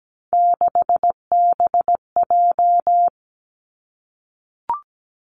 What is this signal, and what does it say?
Morse code, heard as a keyed tone, about 17 wpm, 700 Hz. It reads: 6BJ
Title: Three Challenging Letter-Number Combinations 17wpm